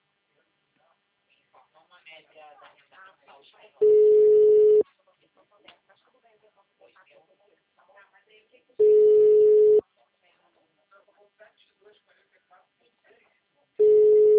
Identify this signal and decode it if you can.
Ringback tone (Europe, 425 Hz), with speech.